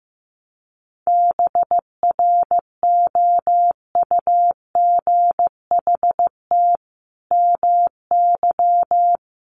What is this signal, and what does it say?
Morse code, heard as a keyed tone, about 15 wpm, 700 Hz. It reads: BROUGHT MY
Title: Sentences from Top 700 Words 15wpm